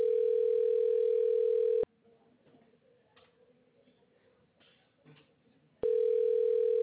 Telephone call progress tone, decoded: ringback tone

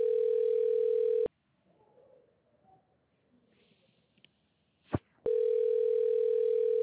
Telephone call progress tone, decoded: ringback tone

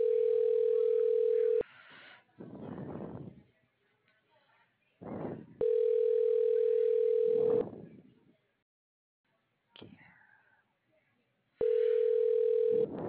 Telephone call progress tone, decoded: ringback tone

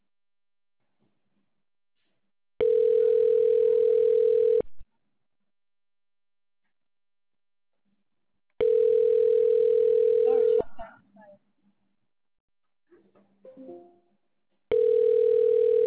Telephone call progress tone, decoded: ringback tone

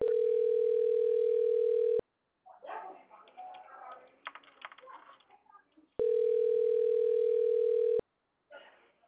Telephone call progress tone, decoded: ringback tone

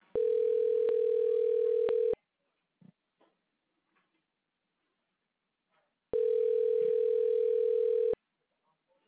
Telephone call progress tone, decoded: ringback tone

